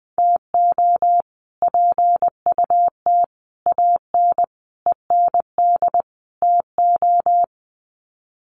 Morse code, heard as a keyed tone, about 20 wpm, 700 Hz. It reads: TO PUT AN END TO
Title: Sets of 5 Words Encore2 20wpm